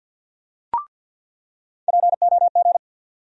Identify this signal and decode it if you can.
Morse code, heard as a keyed tone, about 47 wpm, 700 Hz. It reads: POG
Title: USA Call Sign Postfixes - 3 Letters 50wpm